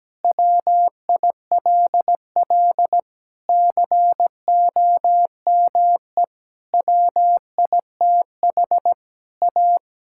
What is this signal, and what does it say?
Morse code, heard as a keyed tone, about 17 wpm, 700 Hz. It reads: WILL COME WITH A
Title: Sentences from Top 1600 Words 17wpm